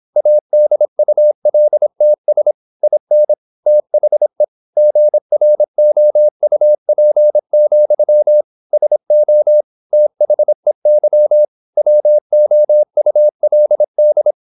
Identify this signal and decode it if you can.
Morse code, heard as a keyed tone, about 26 wpm, 600 Hz. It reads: ADULTS IN THE GROUP, SO THEY WOULD
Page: Never 26wpm | CW med Gnister